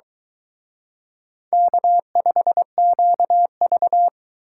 Morse code, heard as a keyed tone, about 24 wpm, 700 Hz. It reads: K5QV